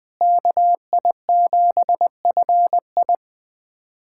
Morse code, heard as a keyed tone, about 20 wpm, 700 Hz. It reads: KI7FI
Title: USA Callsign DE Callsign 20wpm